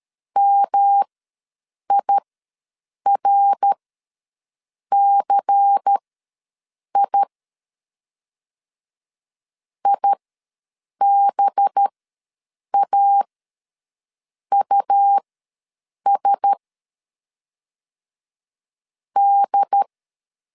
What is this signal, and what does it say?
Morse code, heard as a keyed tone, about 13 wpm, spaced as for 7 wpm, 800 Hz. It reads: MIRCI IBAUS D
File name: LEZIONI / 5 LBVC / 5 LBVC ws 12wpm 3159